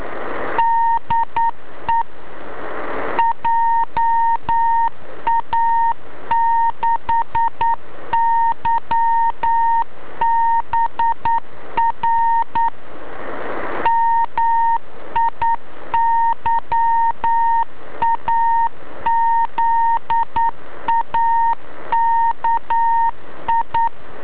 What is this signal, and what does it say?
Morse code, heard as a keyed tone, about 9 wpm, 920 Hz. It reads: DE JA6YBR MIYAZAKI
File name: beacon / ja6ybr